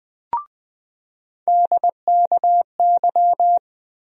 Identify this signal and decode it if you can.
Morse code, heard as a keyed tone, about 21 wpm, 700 Hz. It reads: DKY
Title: USA Call Sign Postfixes - 3 Letters 20wpm